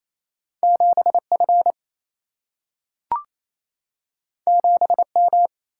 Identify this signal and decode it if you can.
Morse code, heard as a keyed tone, about 28 wpm, 700 Hz. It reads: 7 F 7 M
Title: Single-Letter Single-Number 28wpm